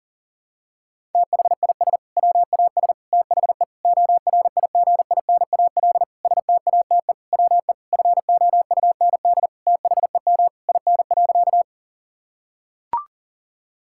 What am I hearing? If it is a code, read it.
Morse code, heard as a keyed tone, about 37 wpm, 700 Hz. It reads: THIS WAS THE ORIGINAL STATE WE FOUND THEM IN.
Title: Sentences from Top 1000 Words - Review Part 6 - 40wpm